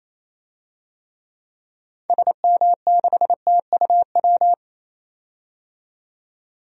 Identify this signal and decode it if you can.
Morse code, heard as a keyed tone, about 28 wpm, 700 Hz. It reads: SM6TUW
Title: DXCC-100 Call Signs - 6 Characters - 28wpm